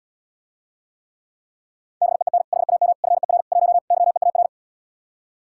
Morse code, heard as a keyed tone, about 80 wpm, 700 Hz. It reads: BEEN HAM FER 1 YEAR